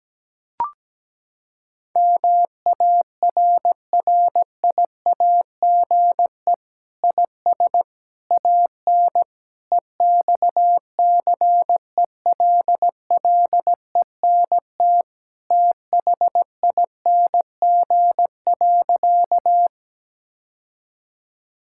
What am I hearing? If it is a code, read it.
Morse code, heard as a keyed tone, about 17 wpm, 700 Hz. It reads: MARRIAGE IS AN EXCELLENT THING.
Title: Sentences from Top 1000 Words - Review Part 5 - 17wpm